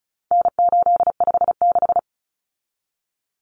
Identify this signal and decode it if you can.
Morse code, heard as a keyed tone, about 34 wpm, 700 Hz. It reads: N856
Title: Common Cut Numbers - 4-Digit 35wpm